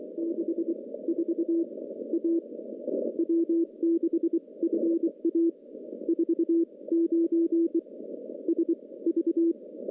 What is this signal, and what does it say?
Morse code, heard as a keyed tone, about 23 wpm, spaced as for 19 wpm, 330 Hz. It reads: 64 A W6FA 49 SV
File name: CW-SS-k5tr